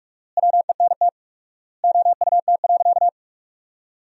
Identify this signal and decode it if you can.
Morse code, heard as a keyed tone, about 43 wpm, 700 Hz. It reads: WENT OUT.
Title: 4x Word-Spacing - Sentences from Top 300 Words - 45wpm